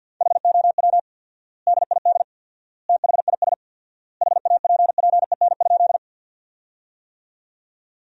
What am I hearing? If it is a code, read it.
Morse code, heard as a keyed tone, about 53 wpm, 700 Hz. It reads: HOW DID THIS HAPPEN?
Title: 4x Word-Spacing - Sentences from Top 400 Words - 50wpm